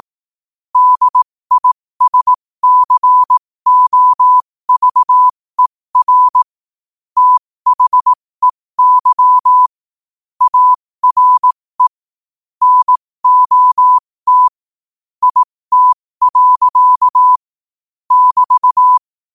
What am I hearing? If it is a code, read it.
Morse code, heard as a keyed tone, about 18 wpm, spaced as for 15 wpm, 1000 Hz. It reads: DISCOVER THEY ARE NOT IT. <BT>